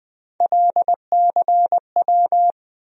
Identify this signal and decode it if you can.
Morse code, heard as a keyed tone, about 20 wpm, 700 Hz. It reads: LCW
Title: USA Call Sign Postfixes - 3 Letters 20wpm